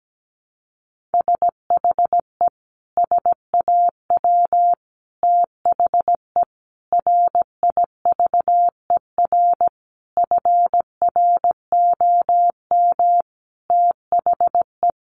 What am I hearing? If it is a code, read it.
Morse code, heard as a keyed tone, about 17 wpm, 700 Hz. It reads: SHE SAW THE RIVER FROM THE
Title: Sentences from Top 1200 Words 17wpm